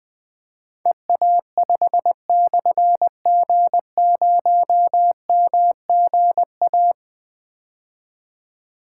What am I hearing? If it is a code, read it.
Morse code, heard as a keyed tone, about 20 wpm, 700 Hz. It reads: EA5/G0MGA
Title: Difficult Call Signs 20wpm